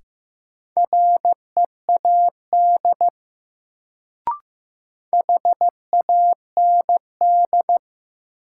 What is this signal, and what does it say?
Morse code, heard as a keyed tone, about 15 wpm, 700 Hz. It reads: READ HAND
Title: Sentences from Top 200 Words - New Word Drill - 15wpm